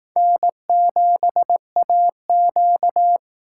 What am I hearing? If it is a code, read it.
Morse code, heard as a keyed tone, about 18 wpm, 700 Hz. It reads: N7AQ
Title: Speed Racing - USA 1x2 Call Signs 15wpm